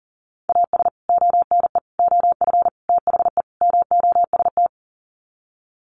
Morse code, heard as a keyed tone, about 37 wpm, 700 Hz. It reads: AS ONE OF THE MOST
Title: Sets of 5 Words 40wpm